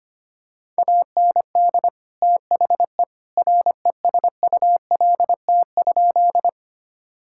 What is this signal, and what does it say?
Morse code, heard as a keyed tone, about 25 wpm, 700 Hz. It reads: AND THE RESULT?
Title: Sentences from Top 1000 Words - Review Part 5 - 25wpm